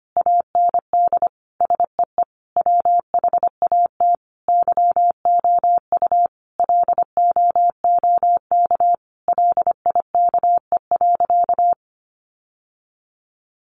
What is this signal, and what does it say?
Morse code, heard as a keyed tone, about 25 wpm, 700 Hz. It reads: AND SEE WHAT YOU LOOK LIKE.
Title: Sentences from Top 1300 Words 25wpm